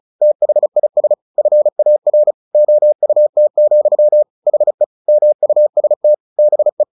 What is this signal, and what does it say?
Morse code, heard as a keyed tone, about 34 wpm, 600 Hz. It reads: THIS FAR OUT, HE MUST BE
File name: the old man and the sea / the old man and the sea 1 35wpm